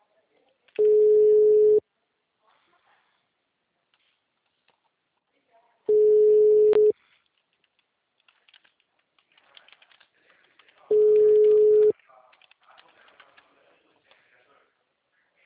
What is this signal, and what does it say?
Ringback tone (Europe, 425 Hz), with speech.